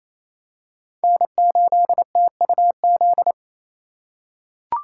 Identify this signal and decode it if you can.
Morse code, heard as a keyed tone, about 28 wpm, 700 Hz. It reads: N8TUZ
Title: USA Call Signs - 5 Characters - 2x Character Spacing - 28wpm